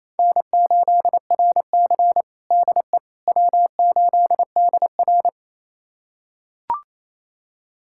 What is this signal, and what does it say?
Morse code, heard as a keyed tone, about 28 wpm, 700 Hz. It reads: N8RC DE W8DR
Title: USA Callsign DE Callsign - 4 Characters - 28wpm